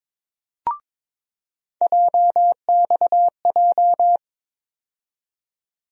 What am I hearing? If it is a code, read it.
Morse code, heard as a keyed tone, about 22 wpm, 700 Hz. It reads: JXJ
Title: USA Call Sign Postfixes - 3 Letters 22wpm